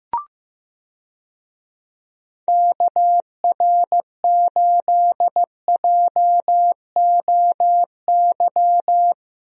Morse code, heard as a keyed tone, about 15 wpm, 700 Hz. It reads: KR8JOY
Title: USA Call Signs - 6 Characters - Repeated 3x - 15wpm